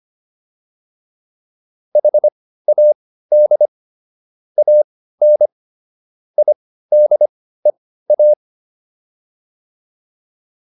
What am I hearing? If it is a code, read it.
Morse code, heard as a keyed tone, about 25 wpm, spaced as for 15 wpm, 600 Hz. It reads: HAD AN IDEA
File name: PR305 15